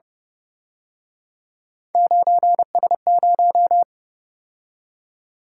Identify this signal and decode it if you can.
Morse code, heard as a keyed tone, about 29 wpm, 700 Hz. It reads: 9S0